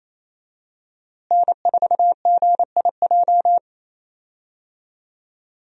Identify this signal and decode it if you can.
Morse code, heard as a keyed tone, about 28 wpm, 700 Hz. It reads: N4GIJ
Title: USA Call Signs - 5 Characters - Repeated 2x - 28wpm